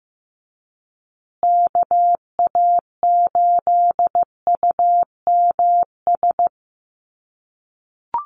Morse code, heard as a keyed tone, about 15 wpm, 700 Hz. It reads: KA8UMS
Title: USA Call Signs Repeated 2x 15wpm